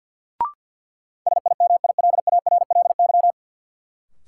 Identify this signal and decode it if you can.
Morse code, heard as a keyed tone, about 49 wpm, 700 Hz. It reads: SIMILARLY
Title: Sentences from Top 1400 Words - New Word Drill - 50wpm